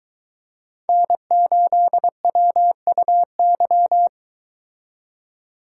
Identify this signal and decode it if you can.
Morse code, heard as a keyed tone, about 22 wpm, 700 Hz. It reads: N8WUY